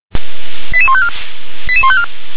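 Five-tone selective call (ZVEI), heard as quick runs of five tones, tones of about 70 ms. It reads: 88154 88154
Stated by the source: Five-tone ZVEI selective call received on an SDR.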